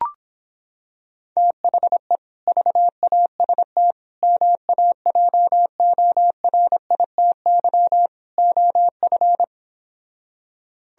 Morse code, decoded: THE VAST MAJORITY OF

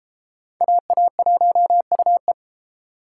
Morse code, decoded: AA1UE

